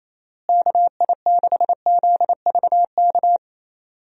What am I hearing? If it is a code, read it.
Morse code, heard as a keyed tone, about 28 wpm, 700 Hz. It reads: KI6ZVK
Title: USA Call Signs - 6 Characters - Repeated 3x - 28wpm